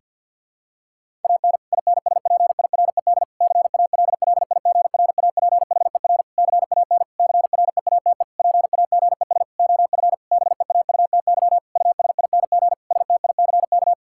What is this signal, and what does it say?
Morse code, heard as a keyed tone, about 49 wpm, 700 Hz. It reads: AN INSPIRED CALLIGRAPHER CAN CREATE PAGES OF BEAUTY USING STICK
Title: Pangram Challenge v2 - 50wpm